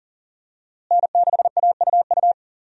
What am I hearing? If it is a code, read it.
Morse code, heard as a keyed tone, about 40 wpm, 700 Hz. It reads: N6AUU
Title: USA Call Signs - 5 Characters - Repeated 3x - 40wpm